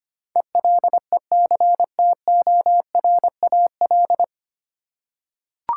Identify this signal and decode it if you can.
Morse code, heard as a keyed tone, about 25 wpm, 700 Hz. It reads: ELECTORAL
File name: practice / 25wpm / 9 Letter Words - First 3 Letters Given - 25wpm